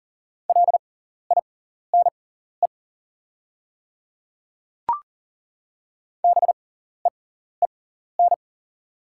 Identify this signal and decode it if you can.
Morse code, heard as a keyed tone, about 37 wpm, spaced as for 14 wpm, 690 Hz. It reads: LINE BEEN
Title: 6x Character Spacing - Top 300 Words - 40wpm